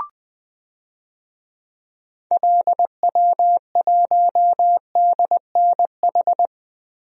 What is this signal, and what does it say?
Morse code, heard as a keyed tone, about 20 wpm, 700 Hz. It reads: LW1DNH
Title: DXCC-100 Call Signs - 6 Characters - Repeated 2x - 20wpm